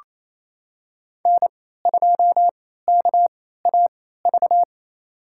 Morse code, heard as a keyed tone, about 28 wpm, spaced as for 16 wpm, 700 Hz. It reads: N2KAV
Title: USA Call Signs - 3x Character Spacing - 28wpm